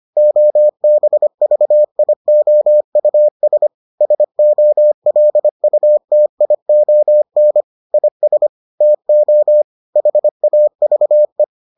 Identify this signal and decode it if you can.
Morse code, heard as a keyed tone, about 25 wpm, 600 Hz. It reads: OBVIOUS SOLUTION IS TO HAVE